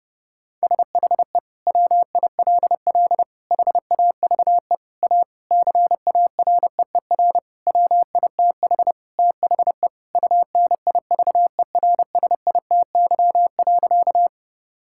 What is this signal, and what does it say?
Morse code, heard as a keyed tone, about 30 wpm, 700 Hz. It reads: SHE WILL HAVE A CAREER WITH THE UNIVERSITY.
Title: Sentences from Top 1000 Words - Review Part 6 - 30wpm